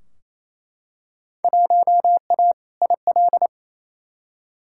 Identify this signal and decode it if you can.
Morse code, heard as a keyed tone, about 28 wpm, 700 Hz. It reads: 1A IL
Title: Field Day Exchanges Encore - 28wpm